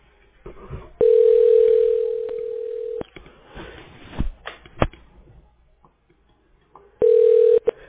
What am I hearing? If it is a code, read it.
Ringback tone (North America), 440 + 480 Hz.